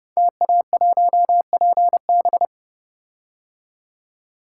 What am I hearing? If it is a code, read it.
Morse code, heard as a keyed tone, about 28 wpm, 700 Hz. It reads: TA1PB
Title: Code Groups - 5 Letter:Numbers - 30wpm